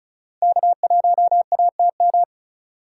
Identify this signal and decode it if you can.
Morse code, heard as a keyed tone, about 48 wpm, spaced as for 42 wpm, 700 Hz. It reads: K1ATM
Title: Speed Racing - USA 1x3 Call Signs 35wpm